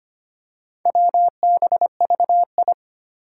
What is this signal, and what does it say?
Morse code, heard as a keyed tone, about 25 wpm, 700 Hz. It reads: WBVI
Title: Four Challenging Letter-Number Combinations 25wpm